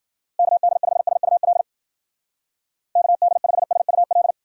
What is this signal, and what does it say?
Morse code, heard as a keyed tone, about 49 wpm, 700 Hz. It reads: KD5SUL KD5SUL
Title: Speed Racing - USA 2x3 Call Signs 40wpm